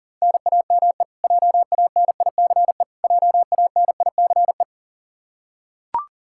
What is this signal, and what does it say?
Morse code, heard as a keyed tone, about 40 wpm, 700 Hz. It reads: NAME JANICE JANICE
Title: Practice QSO - Exchange 1 - Beginning - 40wpm